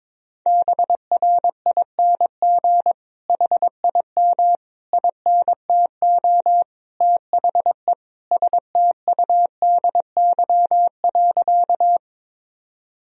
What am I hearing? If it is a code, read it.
Morse code, heard as a keyed tone, about 22 wpm, 700 Hz. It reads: BRING HIM INTO THE STUDY.
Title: Sentences from Top 1000 Words - Review Part 1 - 22wpm